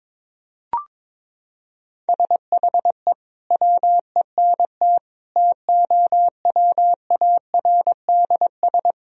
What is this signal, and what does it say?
Morse code, heard as a keyed tone, about 22 wpm, 700 Hz. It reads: SHE WENT TOWARDS
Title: Sentences from Top 1000 Words - Review Part 2 - 22wpm